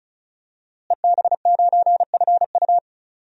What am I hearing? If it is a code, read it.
Morse code, heard as a keyed tone, about 36 wpm, 700 Hz. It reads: EB9FU
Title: DXCC-100 Call Signs - 5 Characters - Repeated 3x - 35wpm